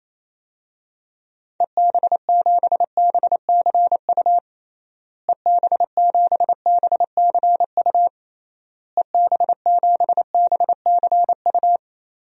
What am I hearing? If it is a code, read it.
Morse code, heard as a keyed tone, about 28 wpm, 700 Hz. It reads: EB7BCU EB7BCU EB7BCU
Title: DXCC-100 Call Signs - 6 Characters - Repeated 3x - 28wpm